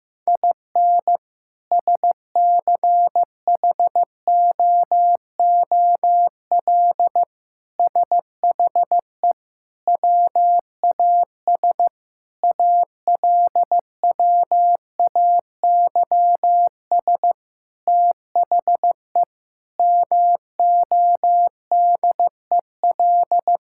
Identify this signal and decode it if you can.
Morse code, heard as a keyed tone, about 16 wpm, 700 Hz. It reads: IN SCHOOL SHE WAS ALWAYS THE MODEL
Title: Sentences from Top 1000 Words - Review Part 5 - 15wpm